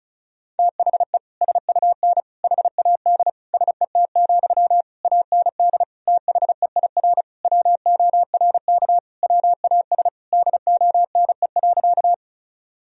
Morse code, heard as a keyed tone, about 36 wpm, 700 Hz. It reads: THE SUN HAD SET, AND THEIR WORK WAS DONE.
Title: Sentences from Top 1000 Words - Review Part 1 - 35wpm